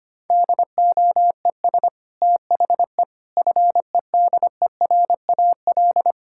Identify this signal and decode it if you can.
Morse code, heard as a keyed tone, about 25 wpm, 700 Hz. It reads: DOES THE FEDERAL
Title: Sentences from Top 1000 Words - Review Part 5 - 25wpm